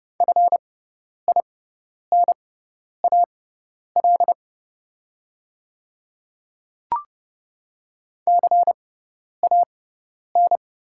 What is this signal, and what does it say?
Morse code, heard as a keyed tone, about 29 wpm, spaced as for 10 wpm, 690 Hz. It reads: FINAL CAN
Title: 6x Character Spacing - Top 500 Words - 30wpm